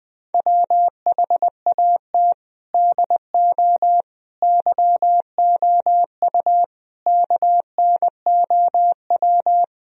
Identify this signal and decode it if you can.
Morse code, heard as a keyed tone, about 20 wpm, 700 Hz. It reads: WHAT DO YOU KNOW
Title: Sentences from Top 1000 Words - Review Part 4 - 20wpm